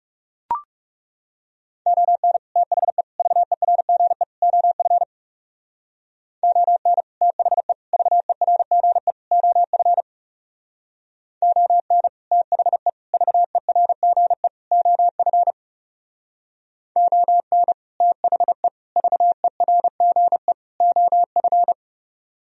Morse code, heard as a keyed tone, about 36 wpm, 700 Hz. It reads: ON THE VERGE OF ON THE VERGE OF ON THE VERGE OF ON THE VERGE OF
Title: Speed-Racing - Sets of 4 Words 30wpm